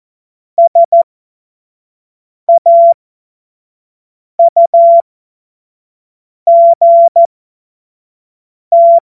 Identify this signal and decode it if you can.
Morse code, heard as a keyed tone, about 14 wpm, spaced as for 5 wpm, 680 Hz. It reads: SAUGT